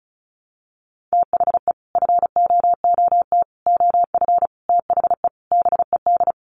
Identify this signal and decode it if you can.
Morse code, heard as a keyed tone, about 34 wpm, 700 Hz. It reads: THE FOOT OF THE BED
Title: Sets of 5 Words Encore2 35wpm